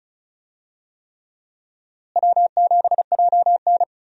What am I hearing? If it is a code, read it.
Morse code, heard as a keyed tone, about 36 wpm, 700 Hz. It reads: W7JN
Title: USA Call Signs - 4 Characters - Repeated 3x - 35wpm